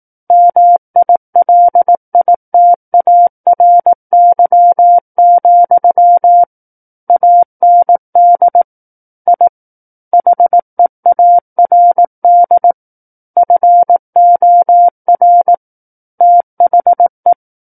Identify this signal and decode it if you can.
Morse code, heard as a keyed tone, about 18 wpm, 700 Hz. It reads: MILITARY, AND I HEARD FOR THE
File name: Book One / 09-Chapter 9 - 18 WPM